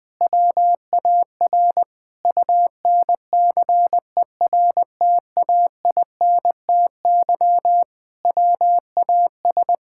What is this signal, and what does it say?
Morse code, heard as a keyed tone, about 21 wpm, 700 Hz. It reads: WAR UNCERTAINTY WAS
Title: Sentences with 6-Word Phrases - 20wpm